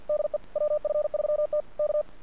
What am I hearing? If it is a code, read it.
Morse code, heard as a keyed tone, about 48 wpm, 600 Hz. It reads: DE WF3T K
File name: archived / CRSnet